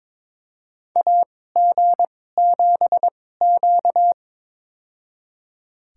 Morse code, heard as a keyed tone, about 22 wpm, spaced as for 16 wpm, 700 Hz. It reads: AG7Q
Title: USA Call Signs - 4 Characters - 2x Character Spacing - 22wpm